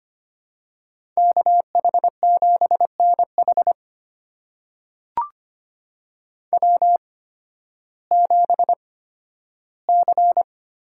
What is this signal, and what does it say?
Morse code, heard as a keyed tone, about 25 wpm, spaced as for 13 wpm, 700 Hz. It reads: KH7NH W7C
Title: USA Call Signs - 5 Characters - 8x Character Spacing - 25wpm